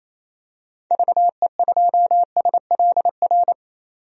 Morse code, heard as a keyed tone, about 28 wpm, 700 Hz. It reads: VE2SLR
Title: DXCC-100 Call Signs - 6 Characters - 28wpm